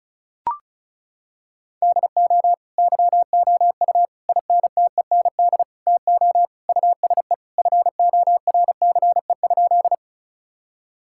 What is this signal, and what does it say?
Morse code, heard as a keyed tone, about 34 wpm, 700 Hz. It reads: DO YOU INTEND TO USE FORCE?
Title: Sentences from Top 1600 Words 35wpm